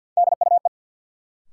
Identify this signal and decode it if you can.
Morse code, heard as a keyed tone, about 52 wpm, 700 Hz. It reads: DUE
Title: Sentences from Top 900 Words - New Word Drill - 50wpm